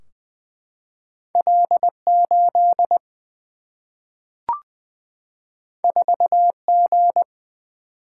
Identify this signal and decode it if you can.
Morse code, heard as a keyed tone, about 20 wpm, 700 Hz. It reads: L 8 4 G
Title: Single-Letter Single-Number 20wpm